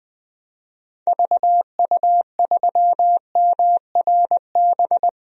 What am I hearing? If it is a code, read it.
Morse code, heard as a keyed tone, about 21 wpm, 700 Hz. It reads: VU3MRB
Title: DXCC-100 Call Signs - 6 Characters - 20wpm